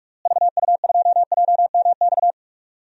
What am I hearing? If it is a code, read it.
Morse code, heard as a keyed tone, about 41 wpm, 700 Hz. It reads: VU2JMX
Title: DXCC-100 Call Signs - 6 Characters - Repeated 3x - 45wpm